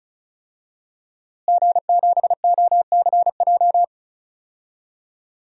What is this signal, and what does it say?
Morse code, heard as a keyed tone, about 36 wpm, 700 Hz. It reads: G7OCJ
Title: DXCC-100 Call Signs - 5 Characters - Repeated 3x - 35wpm